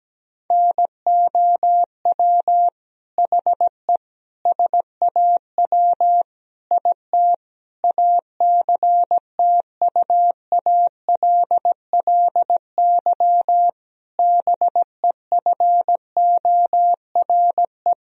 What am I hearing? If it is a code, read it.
Morse code, heard as a keyed tone, about 17 wpm, 700 Hz. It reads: NOW HE SAW IT ACTUALLY BEFORE
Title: Sentences from Top 1000 Words - Review Part 4 - 17wpm